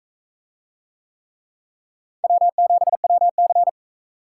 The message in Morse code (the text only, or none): W7WC